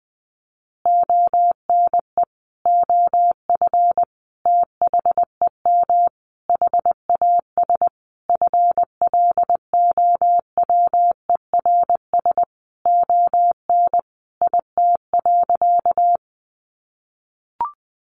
Morse code, heard as a keyed tone, about 20 wpm, 700 Hz. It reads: ONE OF THEM HAS FLOWERS ON IT.
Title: Additional Easy Phrases 20wpm